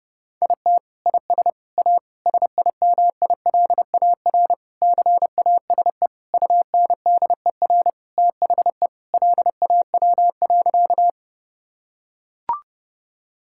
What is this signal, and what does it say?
Morse code, heard as a keyed tone, about 30 wpm, 700 Hz. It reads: IT IS A SIMILAR CASE UNDER THE LAW.
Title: Sentences from Top 1000 Words - Review Part 5 - 30wpm